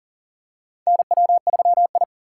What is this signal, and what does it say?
Morse code, heard as a keyed tone, about 40 wpm, 700 Hz. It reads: NW3I
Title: USA Call Signs - 4 Characters - 6x Character Spacing - 40wpm